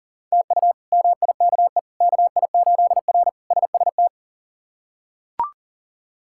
Morse code, heard as a keyed tone, about 37 wpm, 700 Hz. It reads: TU MIKE KI8R SST
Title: K1USN SST QSO Practice - 40wpm